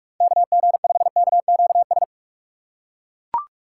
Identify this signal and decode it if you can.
Morse code, heard as a keyed tone, about 41 wpm, 700 Hz. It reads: KG5KQS
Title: USA Call Signs - 6x Character Spacing - 45wpm